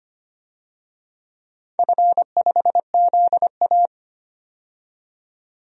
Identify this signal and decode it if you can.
Morse code, heard as a keyed tone, about 25 wpm, 700 Hz. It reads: F5ZA
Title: DXCC-100 Call Signs - 4 Characters - 25wpm